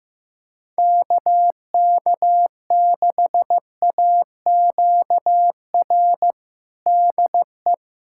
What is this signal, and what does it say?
Morse code, heard as a keyed tone, about 15 wpm, 700 Hz. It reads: KK6AQR DE
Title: USA Callsign DE Callsign 15wpm